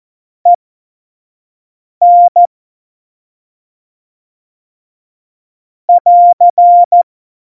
Morse code, heard as a keyed tone, about 14 wpm, spaced as for 5 wpm, 700 Hz. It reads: EN <AR>
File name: Morsekurs / Witze / Witz 17. Oktober 17, Tempo 25